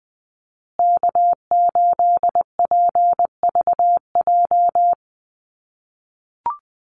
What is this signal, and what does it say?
Morse code, heard as a keyed tone, about 20 wpm, 700 Hz. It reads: K8PVJ
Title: USA Call Signs Repeated 3x 20wpm